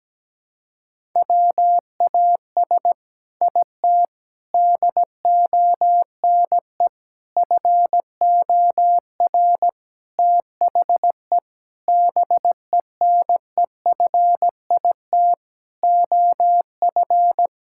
Morse code, heard as a keyed tone, about 17 wpm, 700 Hz. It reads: WAS IT DONE FOR THE BENEFIT OF
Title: Sentences from Top 1000 Words - Review Part 6 - 17wpm